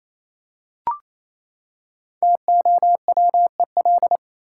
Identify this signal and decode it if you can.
Morse code, heard as a keyed tone, about 28 wpm, 700 Hz. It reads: TOWEL
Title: Five Letter Words - 28wpm